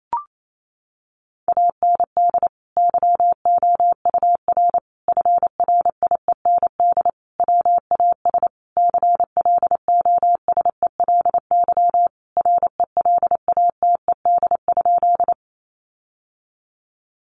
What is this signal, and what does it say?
Morse code, heard as a keyed tone, about 28 wpm, 700 Hz. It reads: AND YOUR FRIEND WAS CLOSELY RELATED?
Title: Sentences from Top 1400 Words 28wpm